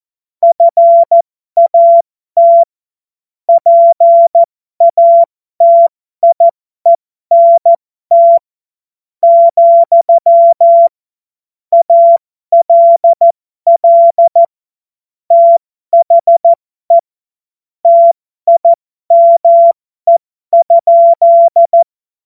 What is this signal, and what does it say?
Morse code, heard as a keyed tone, about 14 wpm, spaced as for 12 wpm, 680 Hz. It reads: FAT PATIENT, ALL THE TIME?
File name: Morsekurs / Witze